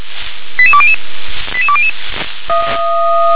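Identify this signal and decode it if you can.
Five-tone selective call (ZVEI), heard as quick runs of five tones, tones of about 70 ms. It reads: 88200 88200
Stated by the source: ZVEI five-tone alert received by SDR.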